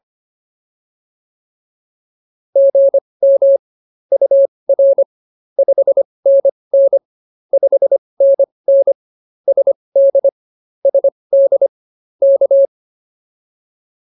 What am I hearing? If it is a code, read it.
Morse code, heard as a keyed tone, about 24 wpm, spaced as for 20 wpm, 550 Hz. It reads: GM UR 5NN 5NN SD SD K